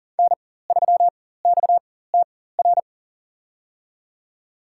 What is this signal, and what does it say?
Morse code, heard as a keyed tone, about 37 wpm, spaced as for 18 wpm, 700 Hz. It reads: N3XTR
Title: USA Call Signs - 4x Character Spacing - 40wpm